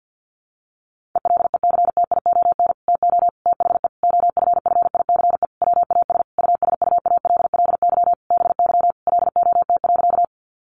Morse code, heard as a keyed tone, about 51 wpm, 700 Hz. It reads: ELECTION TO THE OFFICE WAS USUALLY BY LOT.
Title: Sentences from Top 1500 Words 50wpm